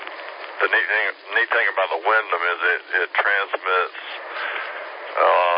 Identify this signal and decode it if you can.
Emergency Alert System (SAME) tones, heard as harsh buzzing bursts, with speech. It is heard from 0.9 to 4.7 s.